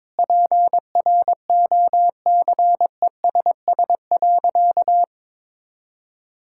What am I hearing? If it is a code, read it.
Morse code, heard as a keyed tone, about 22 wpm, 700 Hz. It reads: PROCESS.
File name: practice / 22wpm / Sentences with 2-Word Phrases - 22wpm